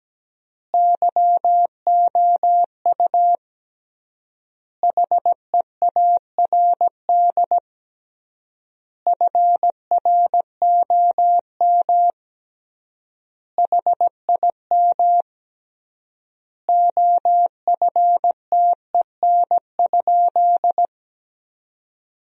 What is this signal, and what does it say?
Morse code, heard as a keyed tone, about 17 wpm, 700 Hz. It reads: YOU HEARD FROM HIM OFTEN?
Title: 3x Word-Spacing - Sentences from Top 400 Words - 17wpm